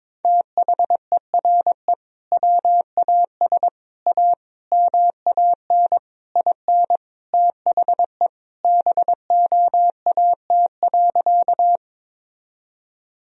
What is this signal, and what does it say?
Morse code, heard as a keyed tone, about 21 wpm, 700 Hz. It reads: THERE WAS A MAN IN THE BOAT.
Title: Sentences from Top 1000 Words - Review Part 3 - 22wpm